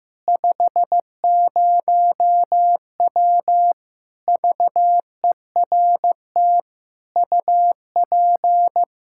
Morse code, heard as a keyed tone, about 15 wpm, 700 Hz. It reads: 50W VERT UP
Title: Practice QSO - Exchange 2 - Info - 15wpm